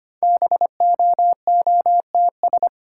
Morse code, heard as a keyed tone, about 25 wpm, 690 Hz. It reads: BOOTS
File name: practice / 25wpm / Easy Words with Hints - 25wpm